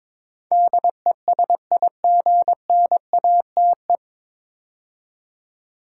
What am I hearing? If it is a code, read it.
Morse code, heard as a keyed tone, about 22 wpm, 700 Hz. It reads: DESIGNATE
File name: 9 Letter Words - First 3 Letters Given - 22wpm